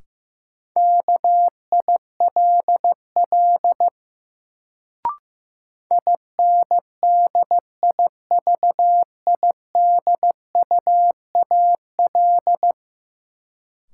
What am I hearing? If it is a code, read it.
Morse code, heard as a keyed tone, about 15 wpm, 700 Hz. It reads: KILL INDIVIDUAL
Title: Sentences from Top 1400 Words - New Word Drill Alt - 15wpm